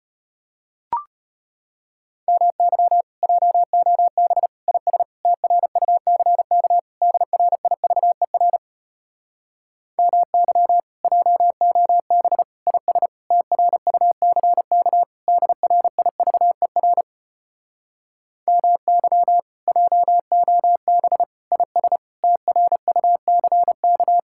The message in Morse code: MY JOB IS TRUCK DRIVER MY JOB IS TRUCK DRIVER MY JOB IS TRUCK

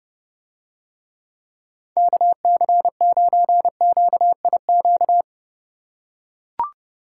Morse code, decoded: KC9QIQ

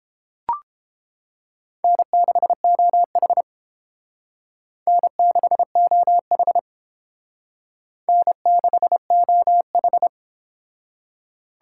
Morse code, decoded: N6OH N6OH N6OH

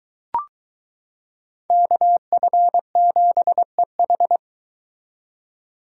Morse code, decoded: KF7EH

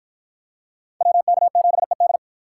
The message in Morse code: WK7ED

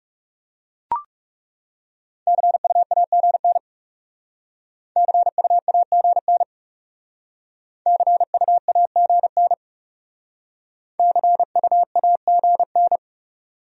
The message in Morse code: CUAGN CUAGN CUAGN CUAGN

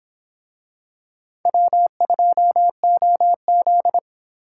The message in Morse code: W2OZ

